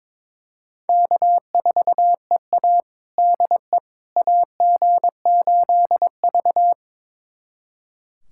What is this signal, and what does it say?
Morse code, heard as a keyed tone, about 22 wpm, 700 Hz. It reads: K4EA DE AG8V
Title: USA Callsign DE Callsign - 4 Characters - 22wpm